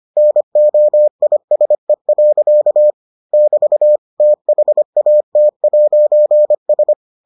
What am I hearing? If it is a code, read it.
Morse code, heard as a keyed tone, about 25 wpm, 600 Hz. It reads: NOISE. <BT> THAT'S